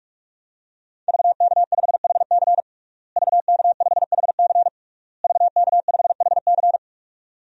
Morse code, heard as a keyed tone, about 46 wpm, 700 Hz. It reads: VK5HC VK5HC VK5HC
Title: DXCC-100 Call Signs - 5 Characters - Repeated 3x - 45wpm